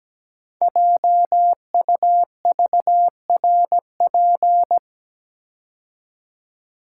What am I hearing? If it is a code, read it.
Morse code, heard as a keyed tone, about 17 wpm, 700 Hz. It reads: JUVRP
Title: Code Groups - 5 Letters - 17wpm